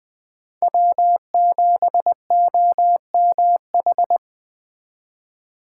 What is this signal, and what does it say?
Morse code, heard as a keyed tone, about 20 wpm, 700 Hz. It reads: W7OMH
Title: Speed Racing - USA 1x3 Call Signs 17wpm